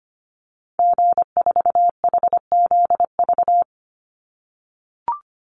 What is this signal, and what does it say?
Morse code, heard as a keyed tone, about 25 wpm, 700 Hz. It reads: G4HZV
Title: Call Signs - 25wpm